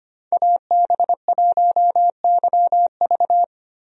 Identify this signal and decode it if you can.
Morse code, heard as a keyed tone, about 25 wpm, 700 Hz. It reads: AB1YV
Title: USA Call Signs - 5 Characters - Repeated 2x - 25wpm